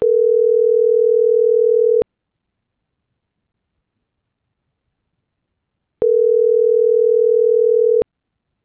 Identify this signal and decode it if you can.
Ringback tone (North America), 440 + 480 Hz.